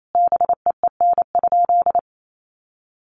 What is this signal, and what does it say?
Morse code, heard as a keyed tone, about 28 wpm, 700 Hz. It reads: BEEN?
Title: Sentences from Top 200 Words 28wpm